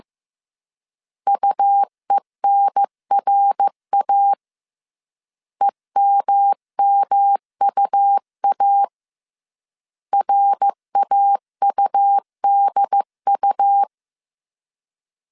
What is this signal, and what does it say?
Morse code, heard as a keyed tone, about 15 wpm, 800 Hz. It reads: UENRA EMMUA RAUDU
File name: CW / DET / 1 3 1500C S C 14wpm 2458